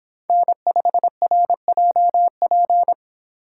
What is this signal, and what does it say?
Morse code, heard as a keyed tone, about 25 wpm, 700 Hz. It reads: N5RJP